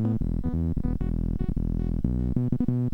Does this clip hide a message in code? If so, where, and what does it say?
No.